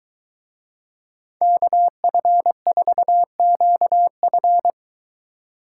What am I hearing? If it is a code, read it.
Morse code, heard as a keyed tone, about 22 wpm, 700 Hz. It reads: KF4QF